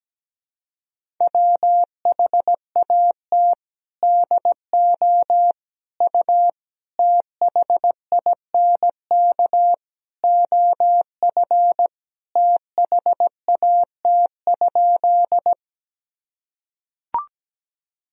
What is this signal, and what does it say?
Morse code, heard as a keyed tone, about 17 wpm, 700 Hz. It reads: WHAT DO U THINK OF THAT?
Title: Sentences from Top 200 Words and Abbreviations 17wpm